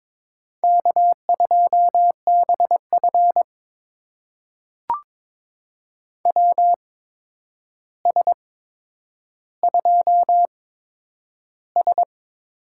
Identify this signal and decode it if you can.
Morse code, heard as a keyed tone, about 22 wpm, spaced as for 9 wpm, 700 Hz. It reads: K2BF WS2S
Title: USA Call Signs - 4 Characters - 8x Character Spacing - 22wpm